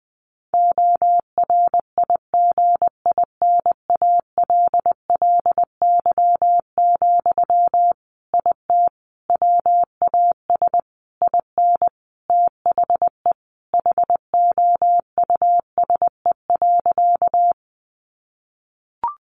Morse code, heard as a keyed tone, about 20 wpm, 700 Hz. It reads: ORIGINALLY, IT WAS IN THE HOUSE.
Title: Sentences from Top 1500 Words 20wpm